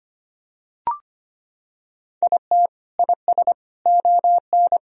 Morse code, heard as a keyed tone, about 25 wpm, 700 Hz. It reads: IT IS ON
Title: Sentences from Top 1000 Words - Review Part 2 - 25wpm